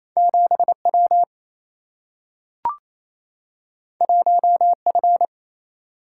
Morse code, heard as a keyed tone, about 28 wpm, 700 Hz. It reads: 7 W 1 F
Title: Single-Letter Single-Number 28wpm